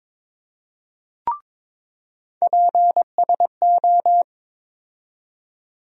Morse code, heard as a keyed tone, about 22 wpm, 700 Hz. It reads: PSO
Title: USA Call Sign Postfixes - 3 Letters 22wpm